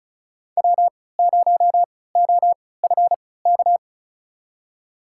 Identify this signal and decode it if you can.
Morse code, heard as a keyed tone, about 34 wpm, spaced as for 20 wpm, 700 Hz. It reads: W0OFK
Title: USA Call Signs - 5 Characters - 3x Character Spacing - 35wpm